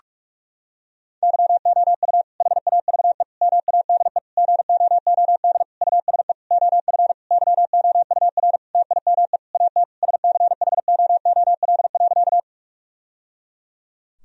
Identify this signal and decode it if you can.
Morse code, heard as a keyed tone, about 46 wpm, 700 Hz. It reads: YOU HAVE MADE GOOD USE OF YOUR TIME AT SCHOOL.
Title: Sentences from Top 1000 Words - Review Part 1 - 45wpm